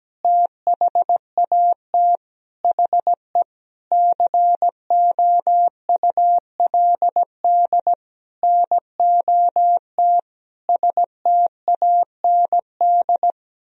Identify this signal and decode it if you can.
Morse code, heard as a keyed tone, about 17 wpm, 700 Hz. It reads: THAT HE COULD NOT STAND
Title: Sentences from Top 1000 Words - Review Part 3 - 17wpm